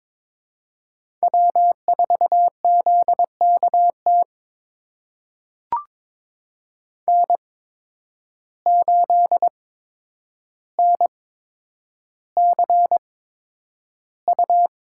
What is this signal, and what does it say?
Morse code, heard as a keyed tone, about 22 wpm, spaced as for 9 wpm, 700 Hz. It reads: W4ZKT N8NCU
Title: USA Call Signs - 5 Characters - 8x Character Spacing - 22wpm